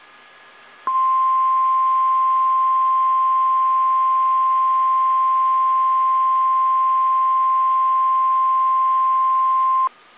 Emergency Alert System (SAME) tones (the 1050 Hz attention tone). It is heard from 0.8 to 9.9 s.
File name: eas / KSDZKDJL / KSDZKDJL